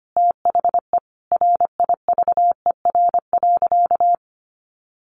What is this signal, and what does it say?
Morse code, heard as a keyed tone, about 25 wpm, 700 Hz. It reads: THE RIVER.
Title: Sentences from Top 1400 Words 25wpm